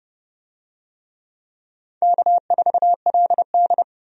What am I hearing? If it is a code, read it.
Morse code, heard as a keyed tone, about 30 wpm, 700 Hz. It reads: K4LD
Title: USA Call Signs - 4 Characters - 30wpm